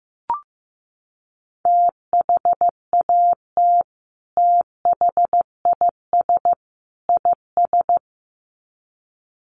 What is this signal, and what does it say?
Morse code, heard as a keyed tone, about 15 wpm, 700 Hz. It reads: THAT THIS IS
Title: Sets of 3 Words 15wpm